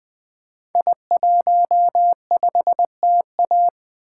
Morse code, heard as a keyed tone, about 20 wpm, 700 Hz. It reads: I15TA
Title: Code Groups - 5 Letter:Numbers - 20wpm